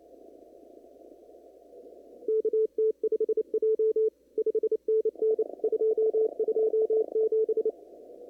Morse code, heard as a keyed tone, about 29 wpm, 440 Hz. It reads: KT5J 5NN 227
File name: CW / kt5j wpx cw-op-k5pi